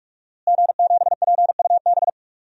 Morse code, heard as a keyed tone, about 42 wpm, 700 Hz. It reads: G7PUB
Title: DXCC-100 Call Signs - 5 Characters - Repeated 3x - 45wpm